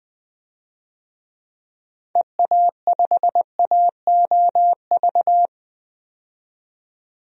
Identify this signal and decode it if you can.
Morse code, heard as a keyed tone, about 21 wpm, 700 Hz. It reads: EA5AOV
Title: DXCC-100 Call Signs - 6 Characters - 20wpm